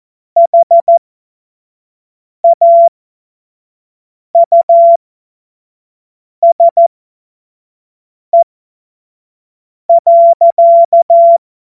Morse code, heard as a keyed tone, about 14 wpm, spaced as for 5 wpm, 680 Hz. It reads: HAUSE.
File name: Morsekurs / Witze / Witz 14. Februar 2023 25 F70